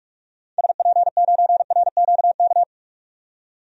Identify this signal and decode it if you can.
Morse code, heard as a keyed tone, about 47 wpm, 700 Hz. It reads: SP9RQK
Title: DXCC-100 Call Signs - 6 Characters - 45wpm